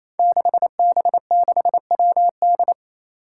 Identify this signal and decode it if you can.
Morse code, heard as a keyed tone, about 27 wpm, 700 Hz. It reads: 6B6WD
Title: Five Challenging Letter-Number Combinations 28wpm